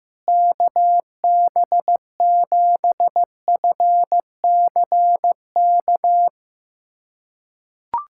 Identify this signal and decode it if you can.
Morse code, heard as a keyed tone, about 15 wpm, 700 Hz. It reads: KB7FCK
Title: USA Callsign DE Callsign - 6 Characters - 15wpm